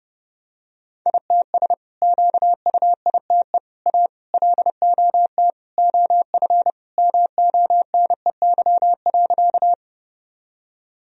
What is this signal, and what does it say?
Morse code, heard as a keyed tone, about 30 wpm, 700 Hz. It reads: ITS QUITE A LOT OF MONEY.
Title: Sentences from Top 1000 Words - Review Part 3 - 30wpm